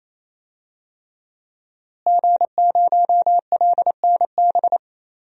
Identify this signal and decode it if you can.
Morse code, heard as a keyed tone, about 27 wpm, 700 Hz. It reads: G0LNB
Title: DXCC-100 Call Signs - 5 Characters - Repeated 3x - 28wpm